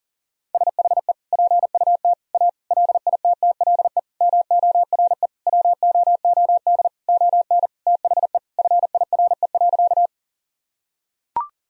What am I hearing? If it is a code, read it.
Morse code, heard as a keyed tone, about 40 wpm, 700 Hz. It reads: SHE PUT A LITTLE MORE WOOD ON THE FIRE.
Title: Sentences from Top 1000 Words - Review Part 2 - 40wpm